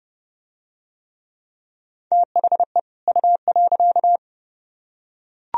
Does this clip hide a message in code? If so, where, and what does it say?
Yes: Morse code from 2.1 to 4.2 s, heard as a keyed tone, about 28 wpm, 700 Hz. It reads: THE U.